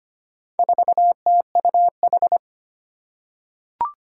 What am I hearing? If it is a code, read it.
Morse code, heard as a keyed tone, about 25 wpm, 700 Hz. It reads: 4TUH
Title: Four Challenging Letter-Number Combinations 25wpm